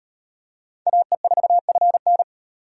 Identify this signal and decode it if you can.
Morse code, heard as a keyed tone, about 37 wpm, 700 Hz. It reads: AE4FN